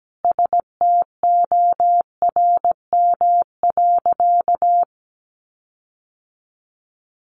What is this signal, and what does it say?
Morse code, heard as a keyed tone, about 17 wpm, 700 Hz. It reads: STORM.
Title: Sentences from Top 1500 Words 17wpm